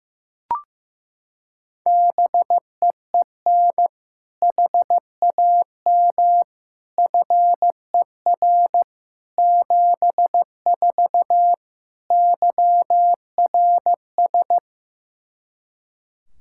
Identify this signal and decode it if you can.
Morse code, heard as a keyed tone, about 16 wpm, 700 Hz. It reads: BEEN HAM FER 74 YRS
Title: K1USN SST QSO Practice 15wpm